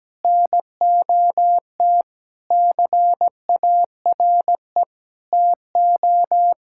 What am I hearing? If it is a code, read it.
Morse code, heard as a keyed tone, about 17 wpm, 700 Hz. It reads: NOT CARE TO
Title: Sentences from Top 1600 Words 17wpm